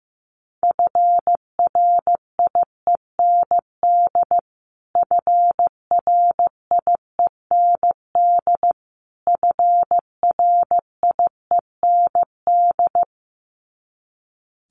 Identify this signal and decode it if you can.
Morse code, heard as a keyed tone, about 15 wpm, 700 Hz. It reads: FRIEND FRIEND FRIEND
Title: Top 300 Words Repeated 15wpm